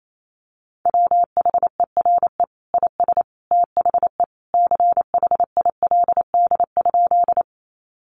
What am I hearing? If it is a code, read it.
Morse code, heard as a keyed tone, about 28 wpm, 700 Hz. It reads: WHERE IS THE CHILD?
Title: Sentences from Top 800 Words 28wpm